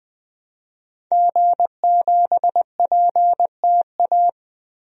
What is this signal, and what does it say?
Morse code, heard as a keyed tone, about 21 wpm, 700 Hz. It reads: G7PTA
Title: DXCC-100 Call Signs - 5 Characters - 20wpm